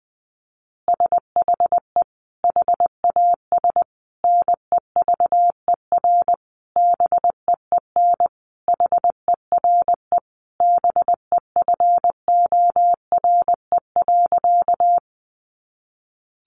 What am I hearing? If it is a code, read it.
Morse code, heard as a keyed tone, about 19 wpm, 700 Hz. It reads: SHE HAS NEVER BEEN HERE BEFORE.
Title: Additional Easy Phrases 20wpm